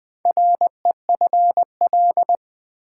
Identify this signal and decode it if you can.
Morse code, heard as a keyed tone, about 20 wpm, 700 Hz. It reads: REFL
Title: Build A Word - 4 Letters At A Time - 10-Letter Words - 20wpm